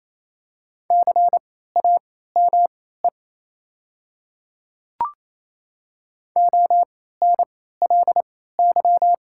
Morse code, heard as a keyed tone, about 27 wpm, spaced as for 16 wpm, 690 Hz. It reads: CAME ONLY
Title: 3x Character Spacing - Top 200 Words - 28wpm